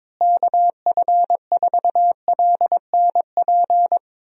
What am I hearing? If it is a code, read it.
Morse code, heard as a keyed tone, about 22 wpm, 700 Hz. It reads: KF4LNP
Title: USA Call Signs - 6 Characters - Repeated 3x - 22wpm